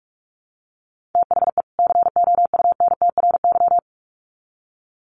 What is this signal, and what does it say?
Morse code, heard as a keyed tone, about 44 wpm, 700 Hz. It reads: THE COUNTRY
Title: Sets of 2 Words 45wpm